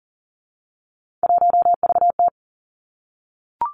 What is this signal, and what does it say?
Morse code, heard as a keyed tone, about 40 wpm, 700 Hz. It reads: 1VT
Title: Three Challenging Letter-Number Combinations 40wpm